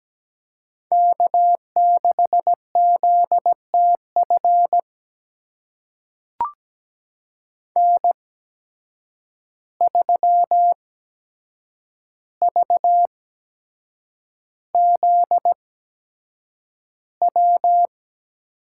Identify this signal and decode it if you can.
Morse code, heard as a keyed tone, about 17 wpm, spaced as for 7 wpm, 700 Hz. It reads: K6ZTF N3VZW
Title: USA Call Signs - 5 Characters - 8x Character Spacing - 17wpm